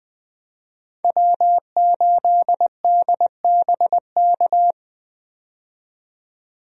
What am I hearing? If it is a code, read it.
Morse code, heard as a keyed tone, about 20 wpm, 700 Hz. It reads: W8DBK
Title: USA Call Signs - 5 Characters - 20wpm